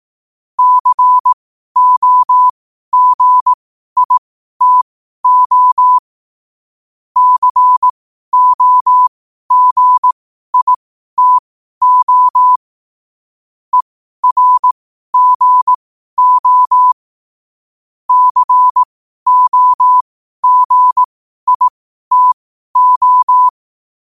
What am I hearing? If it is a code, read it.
Morse code, heard as a keyed tone, about 18 wpm, spaced as for 12 wpm, 1000 Hz. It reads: COGITO COGITO ERGO COGITO
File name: qotdfiles / 12wpm